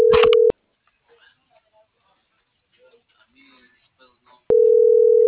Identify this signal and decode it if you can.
Ringback tone (North America), 440 + 480 Hz.